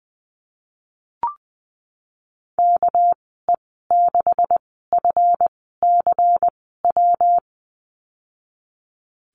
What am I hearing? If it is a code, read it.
Morse code, heard as a keyed tone, about 20 wpm, spaced as for 15 wpm, 700 Hz. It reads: KE6FCW
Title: 207 Lesson 33g - Letter j - Call Signs 20wpm